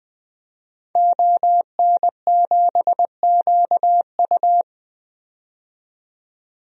Morse code, heard as a keyed tone, about 20 wpm, 700 Hz. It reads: ON7QU
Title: DXCC-100 Call Signs - 5 Characters - 20wpm